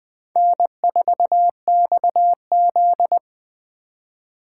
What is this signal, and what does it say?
Morse code, heard as a keyed tone, about 22 wpm, 700 Hz. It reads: N4XZ